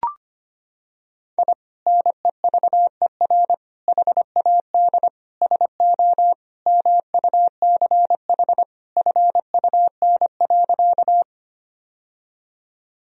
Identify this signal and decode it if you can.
Morse code, heard as a keyed tone, about 25 wpm, 700 Hz. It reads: I NEVER HAD SO MUCH FUN.
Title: Sentences from Top 1000 Words - Review Part 4 - 25wpm